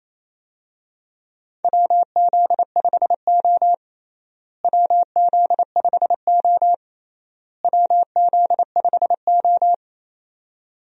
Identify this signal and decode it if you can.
Morse code, heard as a keyed tone, about 27 wpm, 700 Hz. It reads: WZ5O WZ5O WZ5O
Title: USA Call Signs - 4 Characters - Repeated 3x - 28wpm